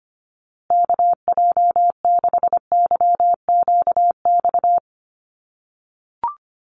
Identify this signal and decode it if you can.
Morse code, heard as a keyed tone, about 25 wpm, 700 Hz. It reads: KJ6YQX
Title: USA Call Signs Repeated 3x 25wpm